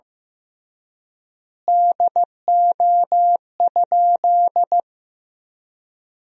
Morse code, heard as a keyed tone, about 15 wpm, 700 Hz. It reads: DO?
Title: 3x Word-Spacing - Sentences from Top 100 Words - 15wpm